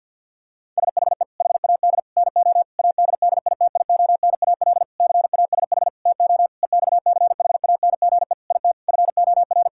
Morse code, heard as a keyed tone, about 50 wpm, 700 Hz. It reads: SHE HAD NO ADDITIONAL CASH TO EXCHANGE IT FOR
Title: Sentences from Top 1000 Words - Review Part 6 - 50wpm